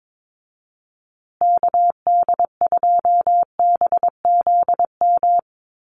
Morse code, heard as a keyed tone, about 22 wpm, 700 Hz. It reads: KD2BZM
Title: USA Call Signs Repeated 2x 22wpm